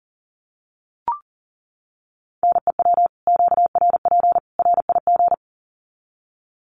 Morse code, heard as a keyed tone, about 40 wpm, 700 Hz. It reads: NEW QRP RIG
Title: Common QSO Phrases 40wpm